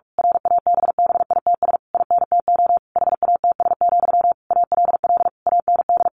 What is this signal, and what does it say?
Morse code, heard as a keyed tone, about 46 wpm, 700 Hz. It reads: RABBITS INTO HATS, ALL AND
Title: A Sound Of Thunder 45wpm